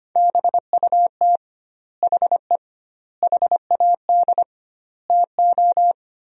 Morse code, heard as a keyed tone, about 25 wpm, 700 Hz. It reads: BUT HE HAD TO
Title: 2x Word-Spacing - Sentences from Top 100 Words - 25wpm